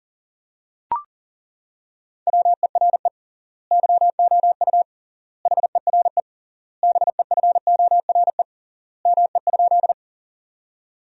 Morse code, heard as a keyed tone, about 40 wpm, 700 Hz. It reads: WERE YOU HERE BEFORE ME?
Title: 3x Word-Spacing - Sentences from Top 200 Words - 40wpm